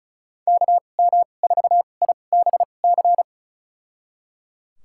Morse code, heard as a keyed tone, about 36 wpm, spaced as for 25 wpm, 700 Hz. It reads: KM4IBC
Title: USA Call Signs - 6 Characters - 2x Character Spacing - 35wpm